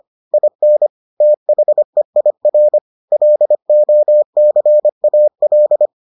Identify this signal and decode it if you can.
Morse code, heard as a keyed tone, about 25 wpm, 600 Hz. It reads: IN THEIR LOCAL